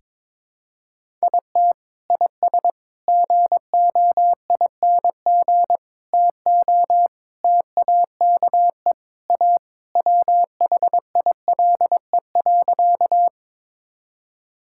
Morse code, heard as a keyed tone, about 21 wpm, 700 Hz. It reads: IT IS GOING TO TAKE A WHILE.
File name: practice / 22wpm / Sentences with 4-Word Phrases - 22wpm